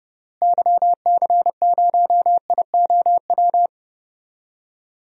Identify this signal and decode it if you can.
Morse code, heard as a keyed tone, about 29 wpm, 700 Hz. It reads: YC0IOW
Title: DXCC-100 Call Signs - 6 Characters - Repeated 3x - 30wpm